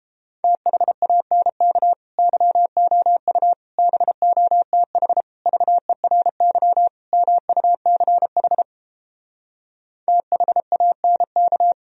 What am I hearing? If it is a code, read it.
Morse code, heard as a keyed tone, about 32 wpm, 700 Hz. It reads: THANK YOU BOTH VERY MUCH THANK